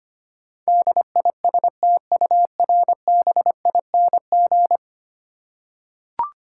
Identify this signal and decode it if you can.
Morse code, heard as a keyed tone, about 25 wpm, 700 Hz. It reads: DISTURBING
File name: 10 Letter Words - First 3 Letters Given - 25wpm